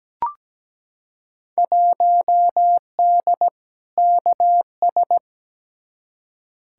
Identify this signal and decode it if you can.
Morse code, heard as a keyed tone, about 17 wpm, 700 Hz. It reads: 1D KS
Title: Field Day Exchanges - 17wpm